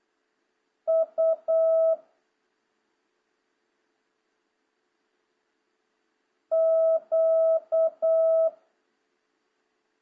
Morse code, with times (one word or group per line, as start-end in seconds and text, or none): 0.9-1.9 U
6.5-8.5 Q